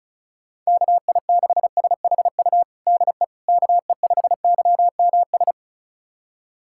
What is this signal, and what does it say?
Morse code, heard as a keyed tone, about 36 wpm, 700 Hz. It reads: KI6SHU DE KE5YMS
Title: USA Callsign DE Callsign - 6 Characters - 35wpm